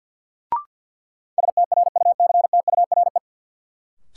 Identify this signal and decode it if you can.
Morse code, heard as a keyed tone, about 47 wpm, 700 Hz. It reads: STRUCTURE
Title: Sentences from Top 1000 Words - Review Part 6 - New Word Drill - 50wpm